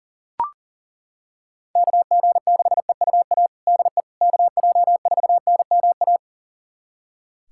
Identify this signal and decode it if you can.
Morse code, heard as a keyed tone, about 40 wpm, 700 Hz. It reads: KG6EUA DE KJ4NMA
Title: USA Callsign DE Callsign - 6 Characters - 40wpm